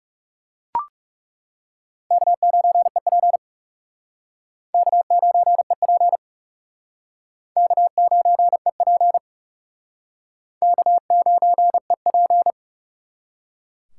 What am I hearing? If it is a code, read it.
Morse code, heard as a keyed tone, about 36 wpm, 700 Hz. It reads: K9EP K9EP K9EP K9EP
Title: Speed Racing - USA 1x2 Call Signs 30wpm